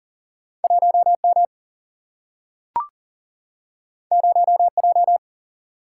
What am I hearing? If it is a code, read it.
Morse code, heard as a keyed tone, about 40 wpm, 700 Hz. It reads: 1M 0J
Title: Single-Letter Single-Number 40wpm